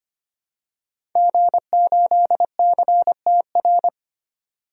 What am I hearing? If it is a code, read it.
Morse code, heard as a keyed tone, about 25 wpm, 700 Hz. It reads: G8CTR
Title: DXCC-100 Call Signs - 5 Characters - 25wpm